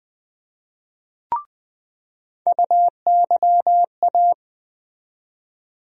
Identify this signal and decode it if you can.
Morse code, heard as a keyed tone, about 20 wpm, 700 Hz. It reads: UYA
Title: USA Call Sign Postfixes - 3 Letters 20wpm